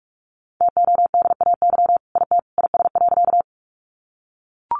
Morse code, heard as a keyed tone, about 44 wpm, 700 Hz. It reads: TODAY IT IS.
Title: Sentences from Top 600 Words 45wpm